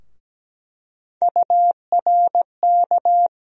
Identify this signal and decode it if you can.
Morse code, heard as a keyed tone, about 17 wpm, 700 Hz. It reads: URK
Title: USA Call Sign Postfixes - 3 Letters 17wpm